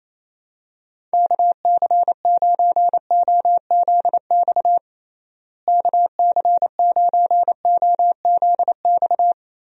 Morse code, heard as a keyed tone, about 28 wpm, 700 Hz. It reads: KC9OZX KC9OZX
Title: USA Call Signs - 6 Characters - Repeated 2x - 28wpm